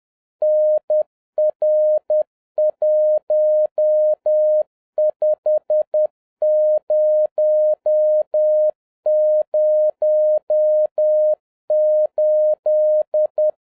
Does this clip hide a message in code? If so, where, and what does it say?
Yes: Morse code from 0.4 to 13.5 s, heard as a keyed tone, about 10 wpm, 610 Hz. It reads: NR15008